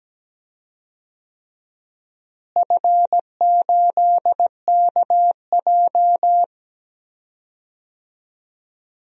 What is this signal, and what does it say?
Morse code, heard as a keyed tone, about 17 wpm, 700 Hz. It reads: F8KJ
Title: DXCC-100 Call Signs - 4 Characters - Repeated 3x - 17wpm